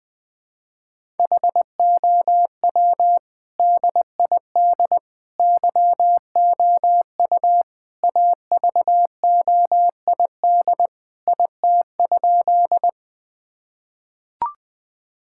Morse code, heard as a keyed tone, about 20 wpm, 700 Hz. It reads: HOW DID YOU AVOID IT?
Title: Sentences from Top 1000 Words - Review Part 6 - 20wpm